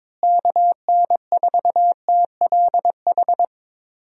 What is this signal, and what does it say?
Morse code, heard as a keyed tone, about 22 wpm, 700 Hz. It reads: KN4TLH
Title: USA Call Signs - 6 Characters - Repeated 3x - 22wpm